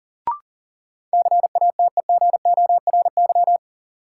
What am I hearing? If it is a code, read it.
Morse code, heard as a keyed tone, about 39 wpm, 700 Hz. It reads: CATEGORY
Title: Sentences from Top 1000 Words - Review Part 7 - New Word Drill - 40wpm